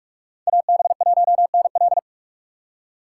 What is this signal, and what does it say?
Morse code, heard as a keyed tone, about 46 wpm, 700 Hz. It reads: AB1NL
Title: USA Call Signs - 3x Character Spacing - 45wpm